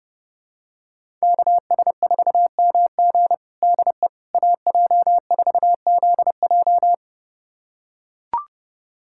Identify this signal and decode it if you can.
Morse code, heard as a keyed tone, about 32 wpm, 700 Hz. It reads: KS4MG DE AJ4ZJ
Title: USA Callsign DE Callsign - 5 Characters - 30wpm